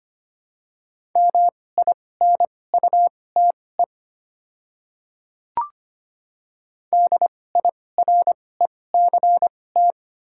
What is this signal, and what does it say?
Morse code, heard as a keyed tone, about 25 wpm, spaced as for 18 wpm, 700 Hz. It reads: MINUTE DIRECT
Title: 2x Character Spacing - Top 500 Words - 25wpm